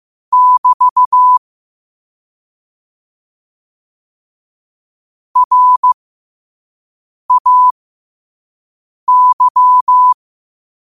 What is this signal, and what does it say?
Morse code, heard as a keyed tone, about 15 wpm, spaced as for 5 wpm, 1000 Hz. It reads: <BT> RAY
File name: qotdfiles / 5wpm